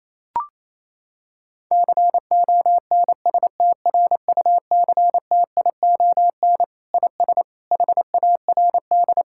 Morse code, heard as a keyed tone, about 28 wpm, 700 Hz. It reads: CONSTRUCTION IS HARD
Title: Sentences from Top 1600 Words 28wpm